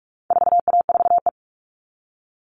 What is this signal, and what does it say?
Morse code, heard as a keyed tone, about 44 wpm, 700 Hz. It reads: 4A4E
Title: All Cut Numbers - 4-Digit 45wpm